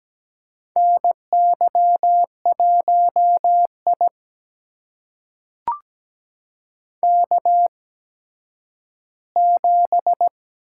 Morse code, heard as a keyed tone, about 17 wpm, spaced as for 10 wpm, 700 Hz. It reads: NY1I K7
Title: USA Call Signs - 4 Characters - 8x Character Spacing - 17wpm